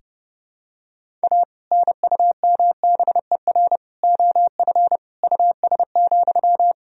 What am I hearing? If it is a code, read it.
Morse code, heard as a keyed tone, about 31 wpm, 700 Hz. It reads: A NUMBER OF US,
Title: Sentences from Top 1000 Words - Review Part 1 - 30wpm